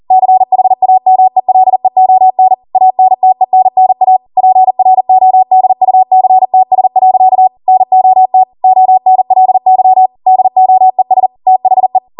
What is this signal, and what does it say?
Morse code, heard as a keyed tone, about 40 wpm, 750 Hz. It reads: CHAMELEON ANTENNA PRODUCTS. NOT ONLY DOES THE